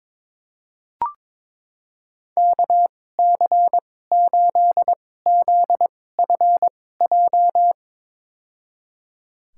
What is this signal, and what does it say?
Morse code, heard as a keyed tone, about 22 wpm, spaced as for 16 wpm, 700 Hz. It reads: KC8ZFJ
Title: USA Call Signs - 2x Character Spacing - 22wpm